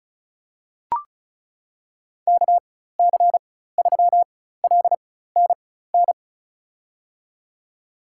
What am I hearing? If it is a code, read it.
Morse code, heard as a keyed tone, about 36 wpm, spaced as for 16 wpm, 700 Hz. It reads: KC3LNN
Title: USA Call Signs - 6 Characters - 4x Character Spacing - 35wpm